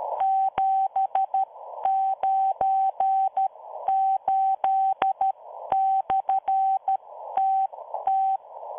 Morse code, heard as a keyed tone, about 13 wpm, spaced as for 11 wpm, 770 Hz. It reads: 798/TT